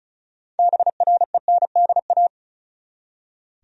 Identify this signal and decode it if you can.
Morse code, heard as a keyed tone, about 36 wpm, 700 Hz. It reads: BRENDA
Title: USA Common Names - 35wpm